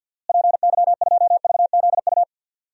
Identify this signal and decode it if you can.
Morse code, heard as a keyed tone, about 52 wpm, 700 Hz. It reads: PY2VZU
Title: DXCC-100 Call Signs - 6 Characters - Repeated 3x - 50wpm